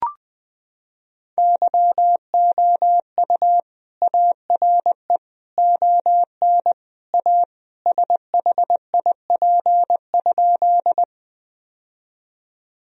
Morse code, heard as a keyed tone, about 21 wpm, 700 Hz. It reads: YOU ARE ON A SHIP?
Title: Sentences from Top 1000 Words - Review Part 2 - 20wpm